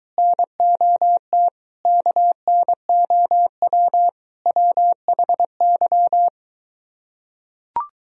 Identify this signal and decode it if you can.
Morse code, heard as a keyed tone, about 23 wpm, 700 Hz. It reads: NOT KNOW WHY